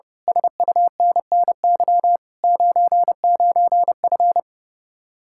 Morse code, heard as a keyed tone, about 28 wpm, 700 Hz. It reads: SUNNY 99F
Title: Speed-Racing Common QSO Phrases - 30wpm